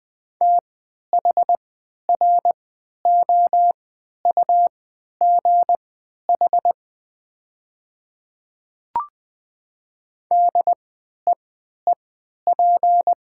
Morse code, heard as a keyed tone, about 20 wpm, spaced as for 11 wpm, 700 Hz. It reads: THROUGH DEEP
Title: 3x Character Spacing - Top 500 Words - 20wpm